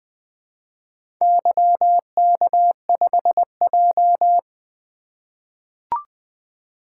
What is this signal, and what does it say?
Morse code, heard as a keyed tone, about 20 wpm, 700 Hz. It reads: YK5J
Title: Four Letter-Number Combinations - 20wpm